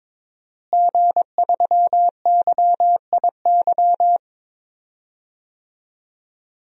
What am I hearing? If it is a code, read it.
Morse code, heard as a keyed tone, about 22 wpm, 700 Hz. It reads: G3YIY
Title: DXCC-100 Call Signs - 5 Characters - Repeated 2x - 22wpm